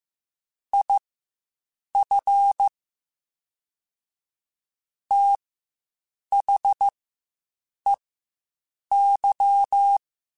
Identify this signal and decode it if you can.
Morse code, heard as a keyed tone, about 15 wpm, spaced as for 7 wpm, 800 Hz. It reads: IF THEY
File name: Quotes Spaced / Quotes Spaced-mp3s / Advice sp @10